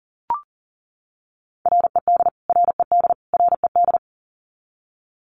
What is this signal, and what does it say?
Morse code, heard as a keyed tone, about 40 wpm, 700 Hz. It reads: RED RED RED
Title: Top 500 Words Repeated 40wpm